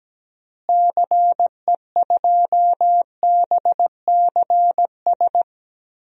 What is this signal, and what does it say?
Morse code, heard as a keyed tone, about 17 wpm, 700 Hz. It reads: CE2BCS
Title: DXCC-100 Call Signs - 6 Characters - Repeated 2x - 17wpm